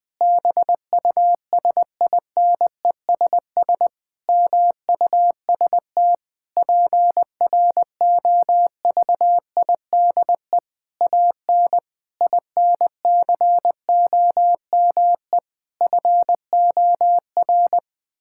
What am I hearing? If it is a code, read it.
Morse code, heard as a keyed tone, about 20 wpm, 700 Hz. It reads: BUSINESS MUST PROVIDE AN INCOME FOR
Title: Sentences from Top 1000 Words - Review Part 5 - 20wpm